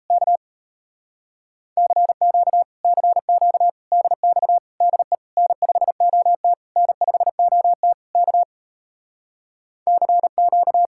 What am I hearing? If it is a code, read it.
Morse code, heard as a keyed tone, about 37 wpm, 700 Hz. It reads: K CQ CQ DX DE N5OT N5OT K CQ